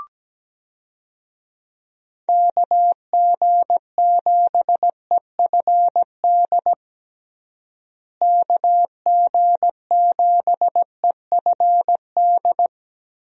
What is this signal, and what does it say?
Morse code, heard as a keyed tone, about 17 wpm, 700 Hz. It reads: KG7EFD KG7EFD
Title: USA Call Signs - 6 Characters - Repeated 3x - 17wpm